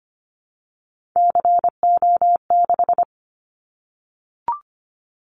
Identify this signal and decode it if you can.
Morse code, heard as a keyed tone, about 25 wpm, 700 Hz. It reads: CO6
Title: Callsign Prefixes for Top 100 DXCC - 25wpm